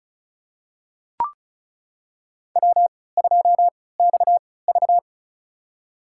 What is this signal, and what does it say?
Morse code, heard as a keyed tone, about 36 wpm, spaced as for 20 wpm, 700 Hz. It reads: W2XV
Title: USA Call Signs - 4 Characters - 3x Character Spacing - 35wpm